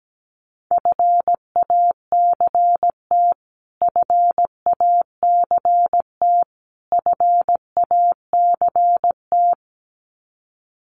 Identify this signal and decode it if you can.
Morse code, heard as a keyed tone, about 17 wpm, 700 Hz. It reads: FACT FACT FACT
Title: Top 500 Words Repeated 17wpm